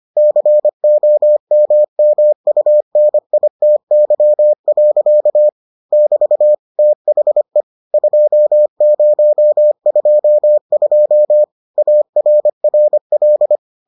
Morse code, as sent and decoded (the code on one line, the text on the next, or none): -.-. --- -- -- ..- -. .. - -.-- .-.-.- / -...- / - .... / ..--- ----- ..--- ..--- / .- .-. .-. .-..
COMMUNITY. <BT> THE 2022 ARRL